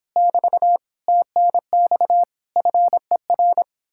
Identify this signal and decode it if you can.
Morse code, heard as a keyed tone, about 29 wpm, 700 Hz. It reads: <BT> TNX FER